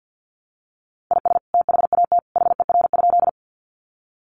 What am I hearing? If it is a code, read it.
Morse code, heard as a keyed tone, about 50 wpm, 700 Hz. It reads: IS THAT HER?
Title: Sentences from Top 100 Words 50wpm